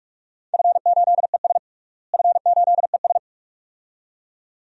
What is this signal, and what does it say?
Morse code, heard as a keyed tone, about 42 wpm, 700 Hz. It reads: F8ES F8ES
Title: DXCC-100 Call Signs - 4 Characters - Repeated 2x - 45wpm